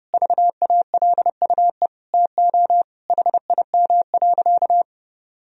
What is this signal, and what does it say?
Morse code, heard as a keyed tone, about 32 wpm, 700 Hz. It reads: VALUE TO HIM.
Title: Sentences from Top 1000 Words - Review Part 3 - 30wpm